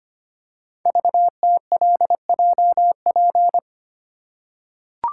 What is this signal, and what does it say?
Morse code, heard as a keyed tone, about 25 wpm, 700 Hz. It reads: VTLJP
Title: Code Groups - 5 Letters - 25wpm